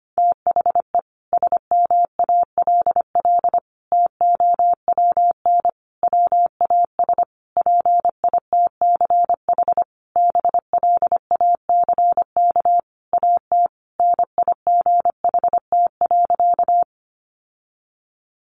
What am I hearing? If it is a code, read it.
Morse code, heard as a keyed tone, about 25 wpm, 700 Hz. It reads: THE SMALL TOWN WAS PITCH BLACK AT NIGHT.
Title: Sentences from Top 1300 Words 25wpm